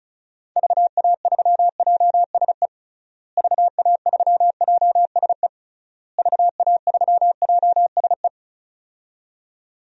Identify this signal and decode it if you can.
Morse code, heard as a keyed tone, about 36 wpm, 700 Hz. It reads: VA3JSE VA3JSE VA3JSE
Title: DXCC-100 Call Signs - 6 Characters - Repeated 3x - 35wpm